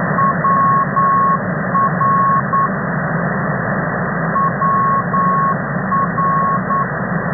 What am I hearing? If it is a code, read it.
Morse code, heard as a keyed tone, about 9 wpm, 1100 Hz, with noise. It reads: WR WR
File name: as ndb